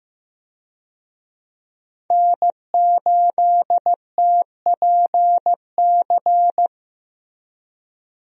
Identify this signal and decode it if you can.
Morse code, heard as a keyed tone, about 15 wpm, 700 Hz. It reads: N8TPC
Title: USA Call Signs - 5 Characters - Repeated 2x - 15wpm